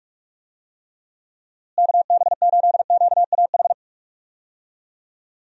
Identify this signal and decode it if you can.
Morse code, heard as a keyed tone, about 42 wpm, 700 Hz. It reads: KB8QAH
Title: USA Call Signs - 6 Characters - Repeated 2x - 45wpm